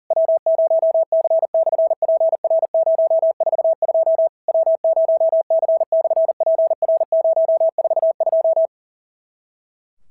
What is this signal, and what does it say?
Morse code, heard as a keyed tone, about 40 wpm, 650 Hz. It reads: W0C/PR042 W0C/PR042
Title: SOTA QSO Practice 40wpm